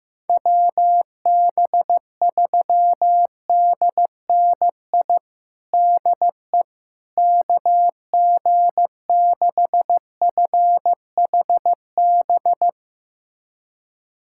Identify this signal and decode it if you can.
Morse code, heard as a keyed tone, about 15 wpm, 700 Hz. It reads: WB3DNI DE KG6FHB
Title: USA Callsign DE Callsign - 6 Characters - 15wpm